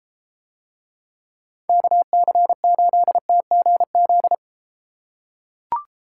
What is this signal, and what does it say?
Morse code, heard as a keyed tone, about 32 wpm, 700 Hz. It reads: KC8TGZ